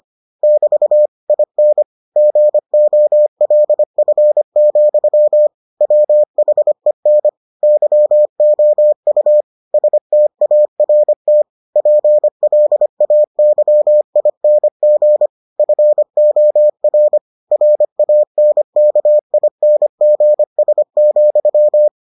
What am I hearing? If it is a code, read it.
Morse code, heard as a keyed tone, about 25 wpm, 600 Hz. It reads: <BT> IN GOLF, WHEN YOU START PLAYING FOR RANKINGS,